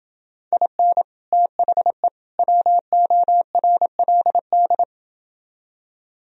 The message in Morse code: IN THE WORLD

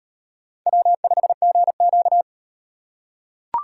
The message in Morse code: W5GQ

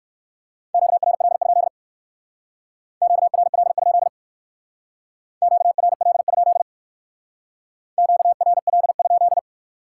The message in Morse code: QRL? QRL? QRL? QRL?

